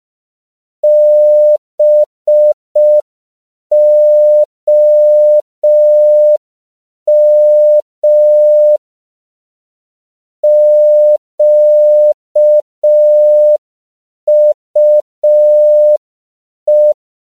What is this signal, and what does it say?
Morse code, heard as a keyed tone, about 5 wpm, 600 Hz. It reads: BOM QUE
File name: CW / 5PPM